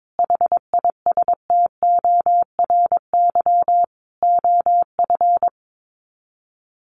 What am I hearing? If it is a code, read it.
Morse code, heard as a keyed tone, about 22 wpm, 700 Hz. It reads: HISTORY OF
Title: Sets of 4 Words 22wpm